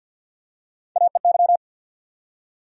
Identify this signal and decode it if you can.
Morse code, heard as a keyed tone, about 52 wpm, 700 Hz. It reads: AEY
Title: USA Call Sign Postfixes - 3 Letters 50wpm